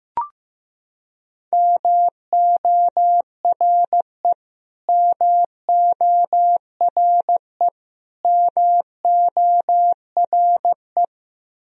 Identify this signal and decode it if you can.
Morse code, heard as a keyed tone, about 15 wpm, 700 Hz. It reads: MORE MORE MORE
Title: Top 100 Words Repeated 15wpm